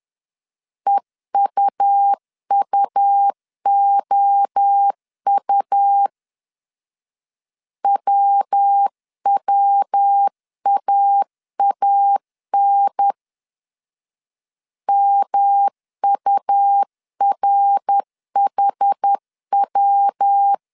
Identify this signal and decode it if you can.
Morse code, heard as a keyed tone, about 10 wpm, 800 Hz. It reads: EUUOU WWAAN MURHW
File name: 10wpm